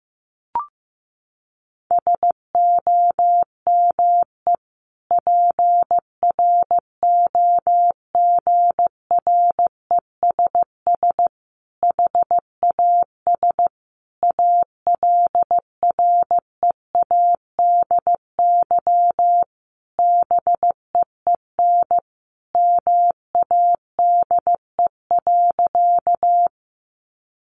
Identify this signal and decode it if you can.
Morse code, heard as a keyed tone, about 15 wpm, 700 Hz. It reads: SOME PROGRESS HAS ALREADY BEEN MADE.
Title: Sentences from Top 1400 Words 15wpm